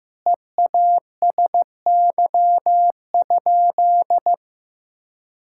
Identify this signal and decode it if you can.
Morse code, heard as a keyed tone, about 15 wpm, 700 Hz. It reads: EASY?
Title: Sentences from Top 1000 Words - Review Part 5 - 15wpm